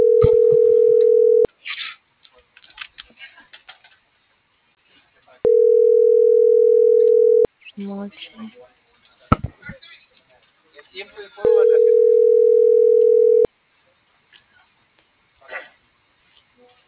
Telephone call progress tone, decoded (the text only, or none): ringback tone